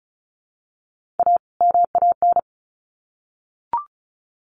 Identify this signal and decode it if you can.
Morse code, heard as a keyed tone, about 34 wpm, 700 Hz. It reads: A MAN
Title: Sets of 2 Words 35wpm